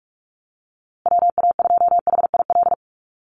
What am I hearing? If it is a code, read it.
Morse code, heard as a keyed tone, about 44 wpm, 700 Hz. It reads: WA2HIL
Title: USA Call Signs Repeated 3x 45wpm